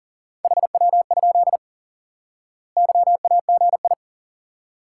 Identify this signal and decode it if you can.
Morse code, heard as a keyed tone, about 40 wpm, 700 Hz. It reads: HW? YAGI
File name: IQE Course 3 - Lesson 89 - YAGI - Two Elements - 40wpm